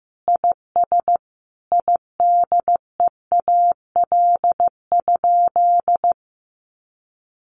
Morse code, heard as a keyed tone, about 15 wpm, 700 Hz. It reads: IS IDEAL?
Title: Sentences from Top 1400 Words 15wpm